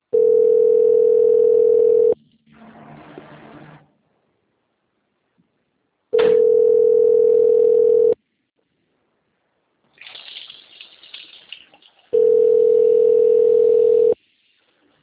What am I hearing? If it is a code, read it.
Ringback tone (North America), 440 + 480 Hz.
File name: out- - - - - .4230